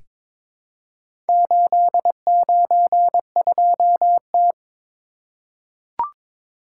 Morse code, heard as a keyed tone, about 23 wpm, 700 Hz. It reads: 892T
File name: practice / 22wpm / Four Letter-Number Combinations - 22wpm